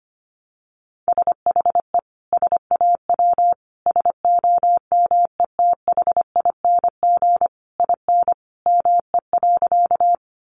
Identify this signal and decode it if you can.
Morse code, heard as a keyed tone, about 25 wpm, 700 Hz. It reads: SHE SAW SOMETHING IN ME.
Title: Sentences from Top 600 Words 25wpm